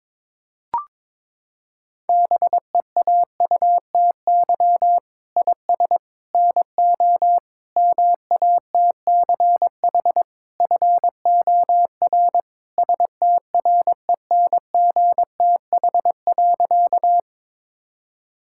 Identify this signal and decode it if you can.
Morse code, heard as a keyed tone, about 23 wpm, 700 Hz. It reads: BEAUTY IS NO MATCH FOR STRENGTH.
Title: Sentences from Top 1600 Words 22wpm